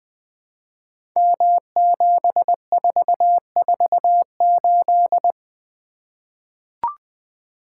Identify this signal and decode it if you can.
Morse code, heard as a keyed tone, about 20 wpm, 700 Hz. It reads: M7448
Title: Code Groups - 5 Letter:Numbers - 20wpm